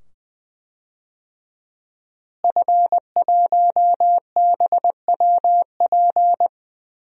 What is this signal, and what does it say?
Morse code, heard as a keyed tone, about 20 wpm, 700 Hz. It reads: F1BWP
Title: DXCC-100 Call Signs - 5 Characters - Repeated 2x - 20wpm